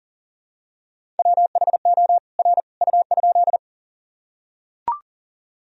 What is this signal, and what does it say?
Morse code, heard as a keyed tone, about 38 wpm, 700 Hz. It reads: WHO R U?
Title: Sentences from Top 100 Words and Abbreviations 40wpm